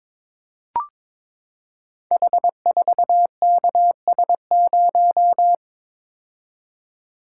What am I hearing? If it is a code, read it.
Morse code, heard as a keyed tone, about 22 wpm, 700 Hz. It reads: H4KS0
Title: Code Groups - 5 Letter:Numbers - 22wpm